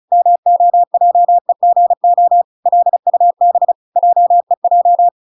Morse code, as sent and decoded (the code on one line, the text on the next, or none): -- --- .--- . --. --- / .-.. ..- -... / .--- . .---
MOJEGO LUB JEJ